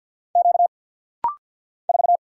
Morse code, heard as a keyed tone, about 50 wpm, 690 Hz. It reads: Q 4
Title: Single Letter-Number - Rapid-Fire 50wpm